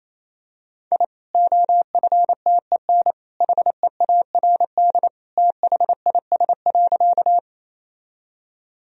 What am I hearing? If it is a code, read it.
Morse code, heard as a keyed tone, about 28 wpm, 700 Hz. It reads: I OFTEN HEARD THIS.
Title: Sentences from Top 1000 Words - Review Part 1 - 28wpm